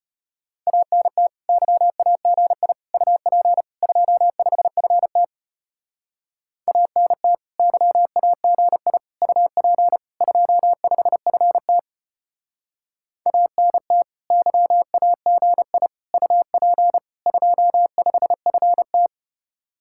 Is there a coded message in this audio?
Yes: Morse code from 0.7 to 19.1 s, heard as a keyed tone, about 33 wpm, 700 Hz. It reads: ANT YAGI UP 25FT ANT YAGI UP 25FT ANT YAGI UP 25FT